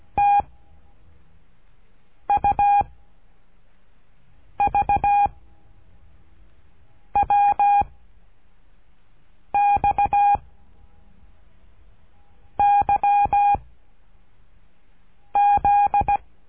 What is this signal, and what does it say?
Morse code, heard as a keyed tone, about 17 wpm, 810 Hz. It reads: T U V W X Y Z
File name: cursos / CURSO DE TELEGRAFIA / Telegrafia - Trilha 13